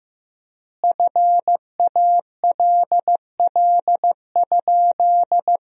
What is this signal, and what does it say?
Morse code, heard as a keyed tone, about 15 wpm, 700 Hz. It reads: FALL?
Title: 4x Word-Spacing - Sentences from Top 500 Words - 15wpm